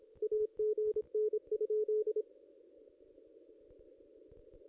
Morse code, heard as a keyed tone, about 26 wpm, 430 Hz. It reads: AGN?
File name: contests / ss-cw-k5tr-op-n5rz -0200